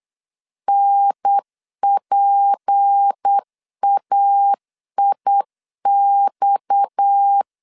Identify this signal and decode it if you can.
Morse code, heard as a keyed tone, about 8 wpm, 800 Hz. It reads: NPAIX
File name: LEZIONI / 6 PKX0 / 6 PKX 8wpm 3041